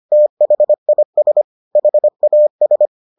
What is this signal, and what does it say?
Morse code, heard as a keyed tone, about 25 wpm, 600 Hz. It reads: THIS HAS